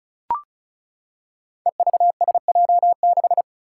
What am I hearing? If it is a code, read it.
Morse code, heard as a keyed tone, about 36 wpm, 700 Hz. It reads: EVSJ6
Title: Five Challenging Letter-Number Combinations 35wpm